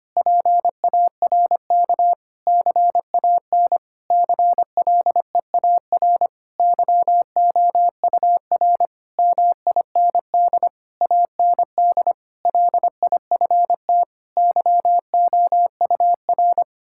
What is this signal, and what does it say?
Morse code, heard as a keyed tone, about 25 wpm, 700 Hz. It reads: PARK CAN CLEAR YOUR MIND AND LIFT YOUR
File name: practice / 25wpm / Difficult Sentences with Hints - 25wpm